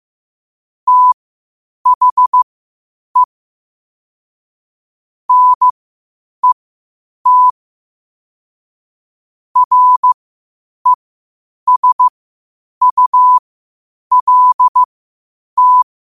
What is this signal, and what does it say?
Morse code, heard as a keyed tone, about 15 wpm, spaced as for 8 wpm, 1000 Hz. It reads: THE NET RESULT
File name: qotdfiles / 8wpm